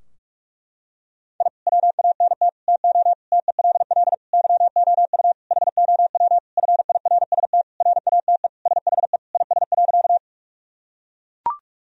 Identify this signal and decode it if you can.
Morse code, heard as a keyed tone, about 41 wpm, 700 Hz. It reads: I WANT TO TELL YOU HOW FIRST RATE SHE IS.
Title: Sentences from Top 1000 Words - Review Part 4 - 45wpm